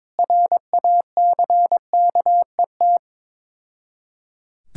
Morse code, heard as a keyed tone, about 23 wpm, 690 Hz. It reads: RACKET
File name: Difficult Words with Hints - 22wpm